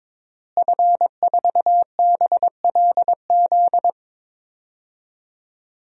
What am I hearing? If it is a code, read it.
Morse code, heard as a keyed tone, about 22 wpm, 700 Hz. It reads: F4BLZ
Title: DXCC-100 Call Signs - 5 Characters - Repeated 3x - 22wpm